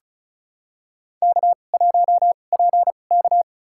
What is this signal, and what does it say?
Morse code, heard as a keyed tone, about 34 wpm, spaced as for 25 wpm, 690 Hz. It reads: K1PK
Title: USA Call Signs - 4 Characters - 2x Character Spacing - 35wpm